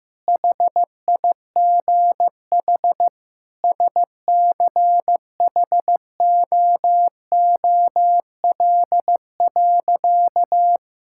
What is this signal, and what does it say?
Morse code, heard as a keyed tone, about 16 wpm, 700 Hz. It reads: HIGH SCHOOL.
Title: Sentences from Top 1000 Words - Review Part 6 - 15wpm